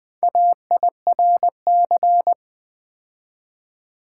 Morse code, heard as a keyed tone, about 21 wpm, 700 Hz. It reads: AIRC
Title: Build A Word - 2 Letters At A Time - 8-Letter Words - 20wpm